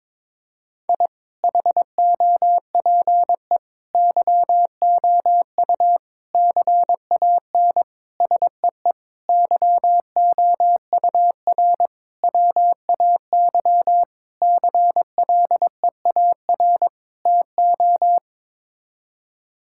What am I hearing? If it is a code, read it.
Morse code, heard as a keyed tone, about 21 wpm, 700 Hz. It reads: I HOPE YOU CAN SEE YOUR WAY CLEAR TO
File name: Sets of 9 Words - 22wpm